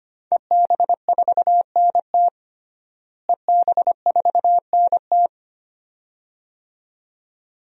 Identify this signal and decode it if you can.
Morse code, heard as a keyed tone, about 25 wpm, 700 Hz. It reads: EB4NT EB4NT
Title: DXCC-100 Call Signs - 5 Characters - Repeated 2x - 25wpm